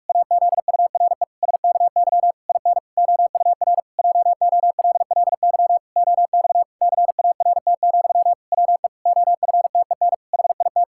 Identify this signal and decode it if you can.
Morse code, heard as a keyed tone, about 46 wpm, 700 Hz. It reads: AZURE SKY IN OUR JOLLY OX CART, WE OFTEN HIT
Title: Pangram Challenge v2 - 45wpm